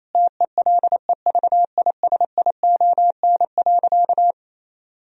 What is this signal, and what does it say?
Morse code, heard as a keyed tone, about 28 wpm, 700 Hz. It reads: TELEVISION.
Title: Sentences from Top 1000 Words - Review Part 4 - 28wpm